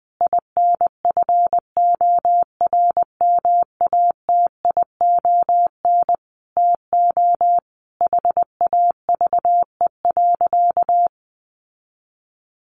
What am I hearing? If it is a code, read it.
Morse code, heard as a keyed tone, about 20 wpm, 700 Hz. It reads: INFORMATION TO HAVE.
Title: Sentences from Top 1400 Words 20wpm